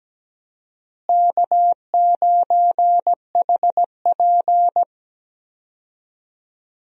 Morse code, heard as a keyed tone, about 17 wpm, 700 Hz. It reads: K9HP
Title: USA Call Signs - 4 Characters - Repeated 2x - 17wpm